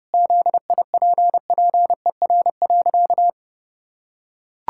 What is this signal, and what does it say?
Morse code, heard as a keyed tone, about 30 wpm, 700 Hz. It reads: ZIPPER.
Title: Pangram Challenge v2 - 30wpm